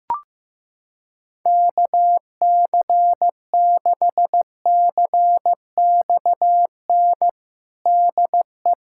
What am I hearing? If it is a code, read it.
Morse code, heard as a keyed tone, about 15 wpm, 700 Hz. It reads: KC6CXN DE
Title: USA Callsign DE Callsign - 6 Characters - 15wpm